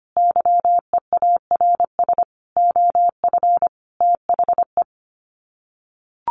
Morse code, heard as a keyed tone, about 25 wpm, 700 Hz. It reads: YEARS OF THE
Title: Sets of 5 Words Encore 25wpm